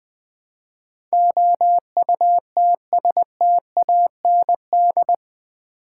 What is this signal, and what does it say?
Morse code, heard as a keyed tone, about 20 wpm, 700 Hz. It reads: OUTSTAND
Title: Build A Word - 4 Letters At A Time - 11-Letter Words - 20wpm